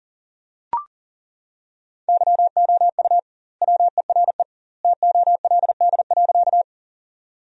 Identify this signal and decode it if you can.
Morse code, heard as a keyed tone, about 40 wpm, 700 Hz. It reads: YOU WERE TOLD.
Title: 2x Word-Spacing - Sentences from Top 400 Words - 40wpm